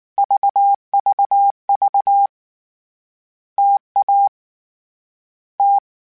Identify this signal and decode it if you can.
Morse code, heard as a keyed tone, about 19 wpm, 800 Hz. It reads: VVV TA T
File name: Difficult Combos / Difficult Combos-mp3s / T AOEI @18